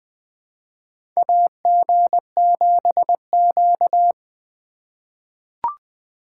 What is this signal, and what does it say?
Morse code, heard as a keyed tone, about 20 wpm, 700 Hz. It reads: AG7Q
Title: USA Call Signs - 4 Characters - 4x Character Spacing - 20wpm